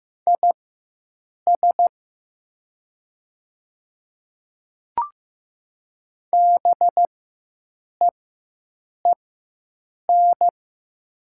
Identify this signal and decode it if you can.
Morse code, heard as a keyed tone, about 15 wpm, spaced as for 7 wpm, 700 Hz. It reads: IS BEEN
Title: 4x Character Spacing - Top 200 Words - 15wpm